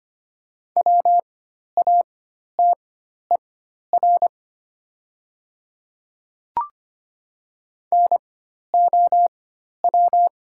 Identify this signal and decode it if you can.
Morse code, heard as a keyed tone, about 25 wpm, spaced as for 12 wpm, 700 Hz. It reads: WATER NOW
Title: 4x Character Spacing - Top 100 Words - 25wpm